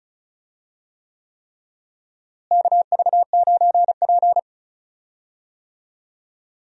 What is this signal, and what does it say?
Morse code, heard as a keyed tone, about 36 wpm, 700 Hz. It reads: KV9P
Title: USA Call Signs - 4 Characters - Repeated 3x - 35wpm